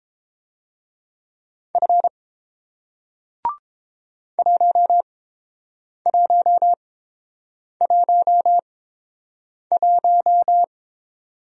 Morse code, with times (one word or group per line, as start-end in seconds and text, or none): 1.8-2.1 F
4.4-10.6 1111